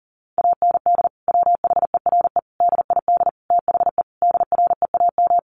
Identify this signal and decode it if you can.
Morse code, heard as a keyed tone, about 40 wpm, 700 Hz. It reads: AND WHERE DID THE DREAM
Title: Sentences from Top 1500 Words 40wpm